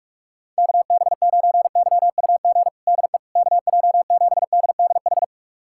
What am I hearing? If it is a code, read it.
Morse code, heard as a keyed tone, about 46 wpm, 700 Hz. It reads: KB9YUG DE KJ7DDH
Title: USA Callsign DE Callsign - 6 Characters - 45wpm